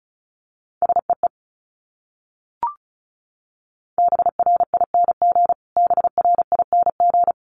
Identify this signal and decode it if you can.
Morse code, heard as a keyed tone, about 34 wpm, 700 Hz. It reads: SEE BRING BRING
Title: Top 500 Words Repeated 35wpm